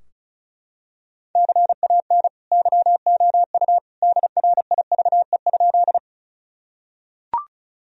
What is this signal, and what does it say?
Morse code, heard as a keyed tone, about 36 wpm, 700 Hz. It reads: CAN YOU DRIVE?
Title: Sentences from Top 1000 Words - Review Part 2 - 35wpm